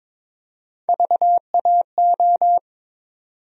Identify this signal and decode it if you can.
Morse code, heard as a keyed tone, about 22 wpm, 700 Hz. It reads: VAO
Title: USA Call Sign Postfixes - 3 Letters 22wpm